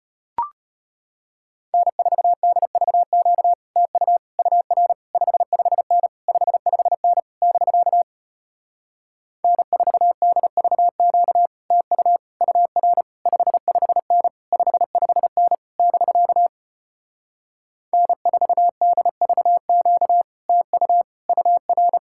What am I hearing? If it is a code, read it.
Morse code, heard as a keyed tone, about 33 wpm, 700 Hz. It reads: N4DVQ TU UR 55N 55N <BK> N4DVQ TU UR 55N 55N <BK> N4DVQ TU UR